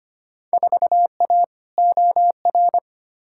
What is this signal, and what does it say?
Morse code, heard as a keyed tone, about 25 wpm, 700 Hz. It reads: 4A OR
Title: Field Day Exchanges - 25wpm